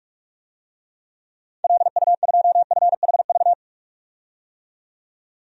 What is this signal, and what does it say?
Morse code, heard as a keyed tone, about 41 wpm, 700 Hz. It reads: LU2FHV
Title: DXCC-100 Call Signs - 6 Characters - 45wpm